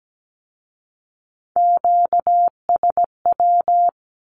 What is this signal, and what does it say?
Morse code, heard as a keyed tone, about 17 wpm, 700 Hz. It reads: QSW
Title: Q-Codes with Spoken Meaning - 17wpm